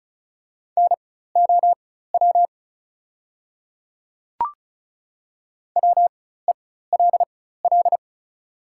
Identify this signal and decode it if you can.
Morse code, heard as a keyed tone, about 36 wpm, spaced as for 16 wpm, 700 Hz. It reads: NOW WELL
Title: 4x Character Spacing - Top 300 Words - 35wpm